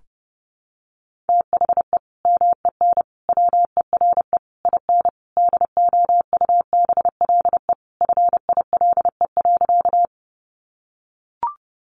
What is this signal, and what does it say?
Morse code, heard as a keyed tone, about 30 wpm, 700 Hz. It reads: THE MEN WERE IN DOUBLE FILE.
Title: Sentences from Top 1400 Words 30wpm